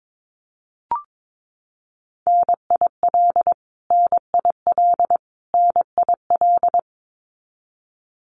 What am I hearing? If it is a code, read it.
Morse code, heard as a keyed tone, about 22 wpm, 700 Hz. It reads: NIL NIL NIL
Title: Abbreviations Repeated 3x - 22wpm